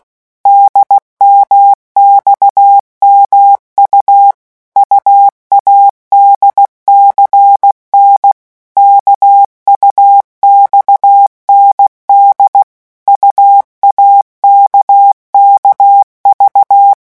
Morse code, heard as a keyed tone, about 16 wpm, 800 Hz. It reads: DMXMU UADCN KUXND UAKKV
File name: morse mp3 / revision3-vitesse 15 mots